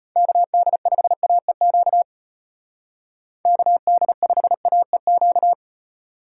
Morse code, heard as a keyed tone, about 32 wpm, 700 Hz. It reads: KD5AEQ KD5AEQ